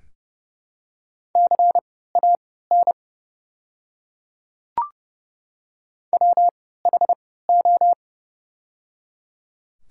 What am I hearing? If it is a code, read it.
Morse code, heard as a keyed tone, about 31 wpm, spaced as for 17 wpm, 700 Hz. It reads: CAN WHO
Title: 3x Character Spacing - Top 500 Words - 30wpm